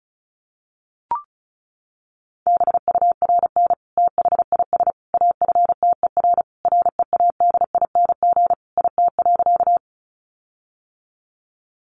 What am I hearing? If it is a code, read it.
Morse code, heard as a keyed tone, about 34 wpm, 700 Hz. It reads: BURN THIS AFTER READING IT.
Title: Sentences from Top 1300 Words 35wpm